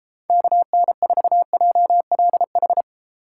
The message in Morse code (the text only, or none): KN4JLH